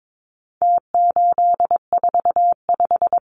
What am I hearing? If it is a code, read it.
Morse code, heard as a keyed tone, about 22 wpm, 700 Hz. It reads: T845
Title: Numbers - Contest Style - 4-Digit 22wpm